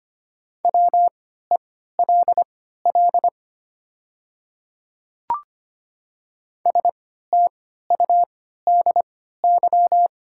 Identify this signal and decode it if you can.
Morse code, heard as a keyed tone, about 25 wpm, spaced as for 14 wpm, 700 Hz. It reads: WELL STUDY
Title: 3x Character Spacing - Top 300 Words - 25wpm